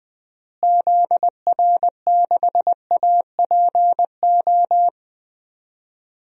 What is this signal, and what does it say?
Morse code, heard as a keyed tone, about 20 wpm, 700 Hz. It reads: ZR6APO
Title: DXCC-100 Call Signs - 6 Characters - Repeated 3x - 20wpm